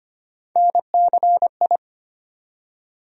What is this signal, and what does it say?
Morse code, heard as a keyed tone, about 25 wpm, 700 Hz. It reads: NCI
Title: USA Call Sign Postfixes - 3 Letters 25wpm